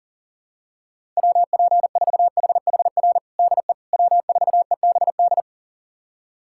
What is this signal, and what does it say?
Morse code, heard as a keyed tone, about 40 wpm, 700 Hz. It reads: WP4HHR DE W4EBD
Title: USA Callsign DE Callsign 40wpm